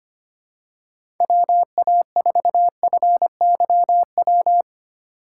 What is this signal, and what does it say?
Morse code, heard as a keyed tone, about 25 wpm, 700 Hz. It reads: WA4FYW
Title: USA Call Signs - 6 Characters - 4x Character Spacing - 25wpm